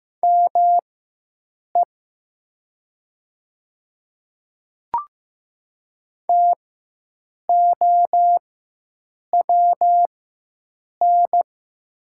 Morse code, heard as a keyed tone, about 15 wpm, spaced as for 7 wpm, 700 Hz. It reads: ME TOWN
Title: 4x Character Spacing - Top 500 Words - 15wpm